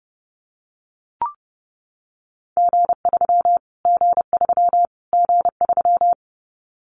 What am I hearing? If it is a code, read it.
Morse code, heard as a keyed tone, about 30 wpm, 700 Hz. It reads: G3 G3 G3
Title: Callsign Prefixes for Top 25 DXCC - Repeated 3x - 30wpm